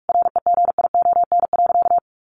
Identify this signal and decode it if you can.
Morse code, heard as a keyed tone, about 44 wpm, 700 Hz. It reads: REGION.
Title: Sentences from Top 1500 Words 45wpm